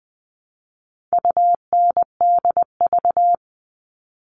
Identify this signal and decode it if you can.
Morse code, heard as a keyed tone, about 20 wpm, 700 Hz. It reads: UNDV
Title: All Cut Numbers - 4-Digit 20wpm